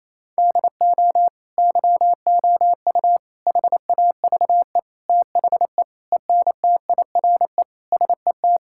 Morse code, heard as a keyed tone, about 28 wpm, 700 Hz. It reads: DO YOU HAVE THE ENTIRE SET
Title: Sentences from Top 1000 Words - Review Part 5 - 28wpm